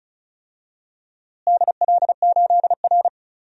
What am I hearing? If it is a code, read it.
Morse code, heard as a keyed tone, about 36 wpm, 700 Hz. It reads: DL8R
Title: DXCC-100 Call Signs - 4 Characters - Repeated 2x - 35wpm